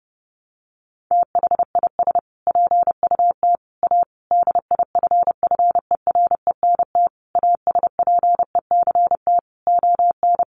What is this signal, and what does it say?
Morse code, heard as a keyed tone, about 30 wpm, 700 Hz. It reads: THIS PUT A DIFFERENT ASPECT ON
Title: Sentences from Top 1300 Words 30wpm